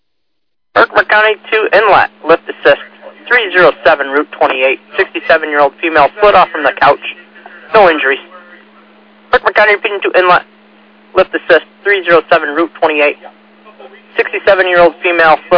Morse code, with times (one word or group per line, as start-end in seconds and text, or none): none